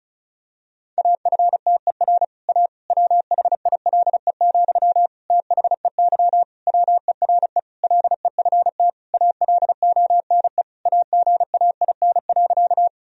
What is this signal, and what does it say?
Morse code, heard as a keyed tone, about 34 wpm, 700 Hz. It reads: AFTER A WHILE, THEY WERE LEFT ALONE AGAIN.
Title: Sentences from Top 1000 Words - Review Part 1 - 35wpm